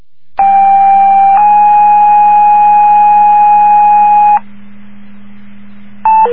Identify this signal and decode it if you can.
Two-tone page, heard as a one-second tone, then a three-second tone, about 790 then 830 Hz. It is heard from 0.4 to 4.4 s.